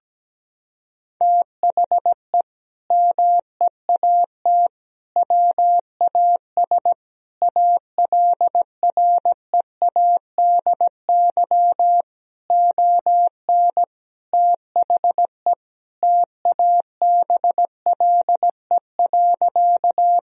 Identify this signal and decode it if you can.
Morse code, heard as a keyed tone, about 17 wpm, 700 Hz. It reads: THE MEAT WAS ALREADY ON THE TABLE.
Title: Sentences from Top 1000 Words - Review Part 3 - 17wpm